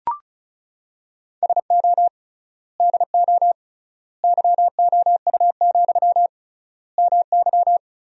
Morse code, heard as a keyed tone, about 34 wpm, 700 Hz. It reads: SO DO YOU, MY
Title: 3x Word-Spacing - Sentences from Top 400 Words - 35wpm